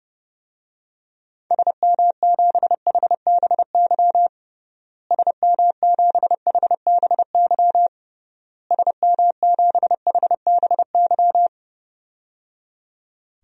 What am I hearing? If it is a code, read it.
Morse code, heard as a keyed tone, about 30 wpm, 700 Hz. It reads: SM7HBY SM7HBY SM7HBY
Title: DXCC-100 Call Signs - 6 Characters - Repeated 3x - 30wpm